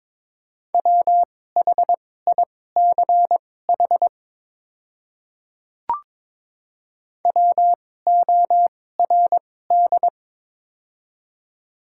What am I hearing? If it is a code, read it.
Morse code, heard as a keyed tone, about 23 wpm, spaced as for 16 wpm, 700 Hz. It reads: WHICH WORD
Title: 2x Character Spacing - Top 100 Words - 22wpm